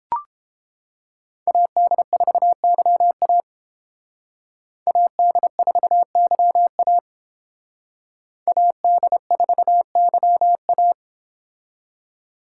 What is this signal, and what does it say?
Morse code, heard as a keyed tone, about 31 wpm, 700 Hz. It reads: AD4YA AD4YA AD4YA